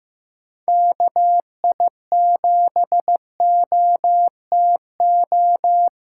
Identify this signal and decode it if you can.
Morse code, heard as a keyed tone, about 15 wpm, 700 Hz. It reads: KI7OTO
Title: USA Call Signs - 6 Characters - 6x Character Spacing - 15wpm